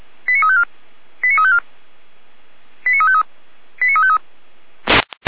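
Five-tone selective call (ZVEI), heard as quick runs of five tones, tones of about 70 ms. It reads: 89354 89354 89353 89353
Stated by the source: ZVEI five-tone call received by SDR, archived by the Feuerwehr Orlamünde fire brigade.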